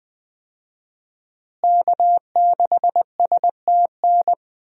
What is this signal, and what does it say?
Morse code, heard as a keyed tone, about 21 wpm, 700 Hz. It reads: K6STN
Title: USA Call Signs - 5 Characters - Repeated 2x - 20wpm